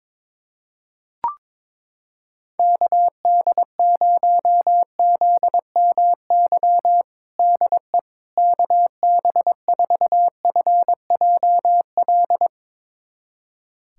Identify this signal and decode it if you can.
Morse code, heard as a keyed tone, about 22 wpm, 700 Hz. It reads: KD0ZMY DE KB4FJL
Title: USA Callsign DE Callsign - 6 Characters - 22wpm